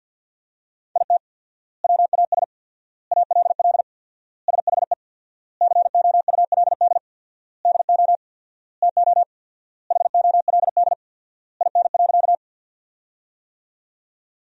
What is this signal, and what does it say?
Morse code, heard as a keyed tone, about 50 wpm, 700 Hz. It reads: IT WAS ALL SHE COULD DO TO HOLD IN.
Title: 4x Word-Spacing - Sentences from Top 400 Words - 50wpm